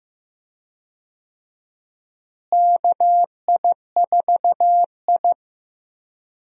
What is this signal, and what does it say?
Morse code, heard as a keyed tone, about 15 wpm, 700 Hz. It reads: KI4I
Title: USA Call Signs - 4 Characters - Repeated 2x - 15wpm